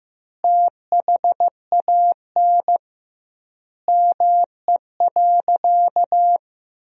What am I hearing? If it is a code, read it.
Morse code, heard as a keyed tone, about 15 wpm, 700 Hz. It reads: THAN ME.
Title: 2x Word-Spacing - Sentences from Top 400 Words - 15wpm